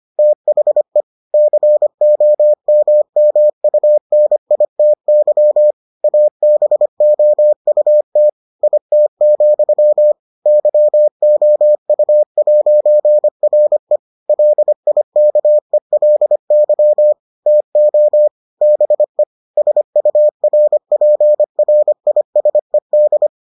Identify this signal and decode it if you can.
Morse code, heard as a keyed tone, about 25 wpm, 600 Hz. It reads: THE COMMUNITY ABOUT IT, YOU'RE LIKELY TO BE SURPRISED